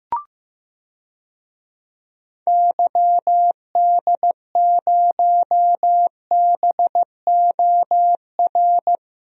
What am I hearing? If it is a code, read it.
Morse code, heard as a keyed tone, about 15 wpm, 700 Hz. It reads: YD0BOR
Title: DXCC-100 Call Signs - 6 Characters - 15wpm